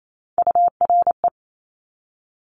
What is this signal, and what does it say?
Morse code, heard as a keyed tone, about 28 wpm, 700 Hz. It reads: URE
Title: Top Three Letter Combinations in Words 28wpm